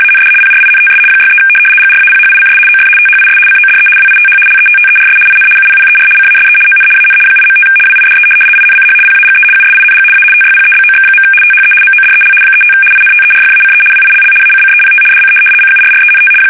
RTTY, about 75 baud. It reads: AND PRINTS 63 GRAPHICS OF THIS CODE WITH LETTERS IN UPPER CASE ON AN 8.5 WIDE INCH PAGE USING A TYPEBOX. THE MODEL 35 INTERFACE WILL ACCEPT DC CURRENT
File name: fsk850-75B / teletype-corp